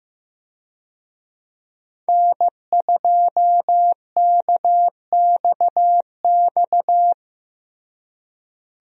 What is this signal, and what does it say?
Morse code, heard as a keyed tone, about 15 wpm, 700 Hz. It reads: N2KXX
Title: USA Call Signs - 5 Characters - Repeated 3x - 15wpm